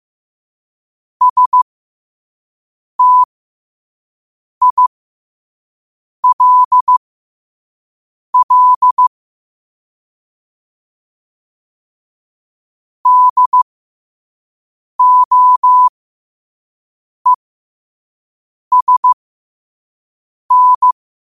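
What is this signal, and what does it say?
Morse code, heard as a keyed tone, about 15 wpm, spaced as for 5 wpm, 1000 Hz. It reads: STILL DOESN
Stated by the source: Morse code practice at 5 wpm.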